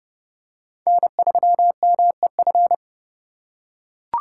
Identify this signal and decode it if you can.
Morse code, heard as a keyed tone, about 31 wpm, 700 Hz. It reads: N3MEF
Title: USA Call Signs - 5 Characters - 4x Character Spacing - 30wpm